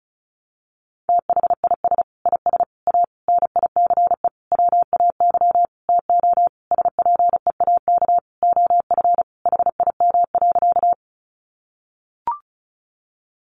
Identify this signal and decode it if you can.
Morse code, heard as a keyed tone, about 34 wpm, 700 Hz. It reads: THIS IS A NICE WAY TO SPEAK OF HIM.
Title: Sentences from Top 1100 Words 35wpm